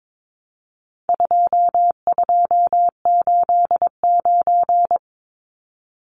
Morse code, heard as a keyed tone, about 22 wpm, 700 Hz. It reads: 2289
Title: Numbers - 4-Digit 22wpm